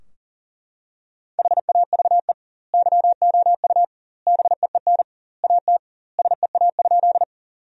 Morse code, heard as a keyed tone, about 40 wpm, 700 Hz. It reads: HAVE YOU BEEN AT SEA?
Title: 2x Word-Spacing - Sentences from Top 300 Words - 40wpm